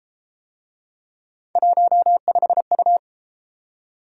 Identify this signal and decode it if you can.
Morse code, heard as a keyed tone, about 30 wpm, 700 Hz. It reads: 15U